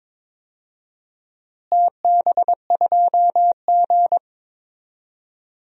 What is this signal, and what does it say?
Morse code, heard as a keyed tone, about 22 wpm, 700 Hz. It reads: TB2G
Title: DXCC-100 Call Signs - 4 Characters - Repeated 3x - 22wpm